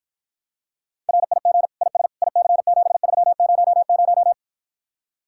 Morse code, heard as a keyed tone, about 53 wpm, 700 Hz. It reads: RIG IS IC7300